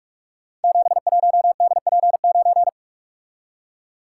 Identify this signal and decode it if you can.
Morse code, heard as a keyed tone, about 44 wpm, 700 Hz. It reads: 71DP9
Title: Code Groups - 5 Letter:Numbers - 45wpm